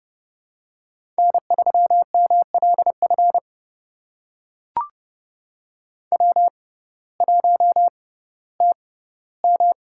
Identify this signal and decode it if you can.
Morse code, heard as a keyed tone, about 30 wpm, spaced as for 16 wpm, 700 Hz. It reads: N3MLF W1TM
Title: USA Call Signs - 6x Character Spacing - 30wpm